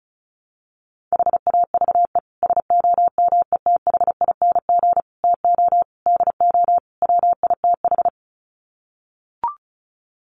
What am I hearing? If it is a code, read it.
Morse code, heard as a keyed tone, about 34 wpm, 700 Hz. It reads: HAVE SOMETHING TO DO WITH
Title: Sets of 5 Words Encore 35wpm